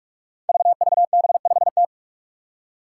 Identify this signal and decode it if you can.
Morse code, heard as a keyed tone, about 46 wpm, 700 Hz. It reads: VVB5T
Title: Five Challenging Letter-Number Combinations 45wpm